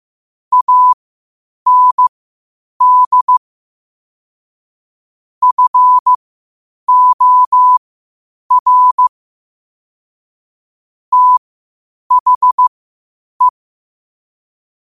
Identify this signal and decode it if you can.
Morse code, heard as a keyed tone, about 15 wpm, spaced as for 8 wpm, 1000 Hz. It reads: AND FOR THE
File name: qotdfiles / 8wpm